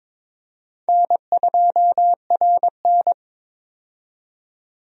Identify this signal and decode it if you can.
Morse code, heard as a keyed tone, about 22 wpm, 700 Hz. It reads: N2RN
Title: USA Call Signs - 4 Characters - 22wpm